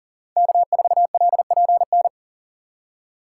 Morse code, heard as a keyed tone, about 40 wpm, 700 Hz. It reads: K4LPN
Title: USA Call Signs - 5 Characters - 2x Character Spacing - 40wpm